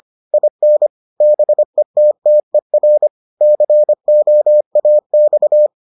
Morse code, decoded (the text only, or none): IN BETTER COAX